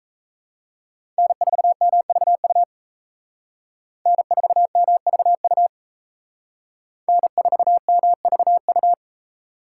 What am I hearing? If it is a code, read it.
Morse code, heard as a keyed tone, about 33 wpm, 700 Hz. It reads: N4MVU N4MVU N4MVU
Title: Speed Racing - USA 1x3 Call Signs 28wpm